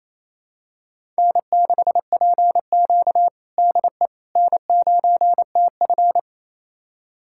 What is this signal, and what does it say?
Morse code, heard as a keyed tone, about 28 wpm, 700 Hz. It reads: N6PQ DE N9TF
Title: USA Callsign DE Callsign - 4 Characters - 28wpm